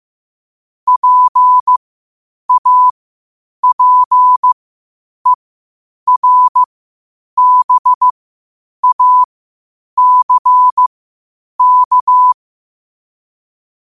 Morse code, heard as a keyed tone, about 15 wpm, spaced as for 8 wpm, 1000 Hz. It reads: PAPERBACK
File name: qotdfiles / 8wpm